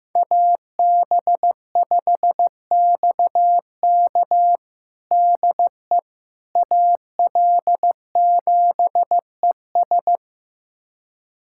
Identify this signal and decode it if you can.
Morse code, heard as a keyed tone, about 15 wpm, 700 Hz. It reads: AB5XK DE AL7ES
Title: USA Callsign DE Callsign - 5 Characters - 15wpm